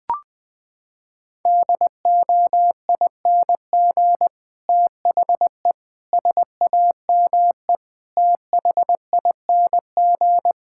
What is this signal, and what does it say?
Morse code, heard as a keyed tone, about 20 wpm, 700 Hz. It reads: DOING THE SAME THING
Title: Sentences with 8-Word Phrases - 20wpm